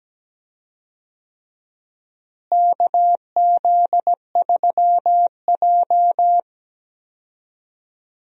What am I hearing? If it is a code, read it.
Morse code, heard as a keyed tone, about 17 wpm, 700 Hz. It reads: KZ3J
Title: USA Call Signs - 4 Characters - Repeated 3x - 17wpm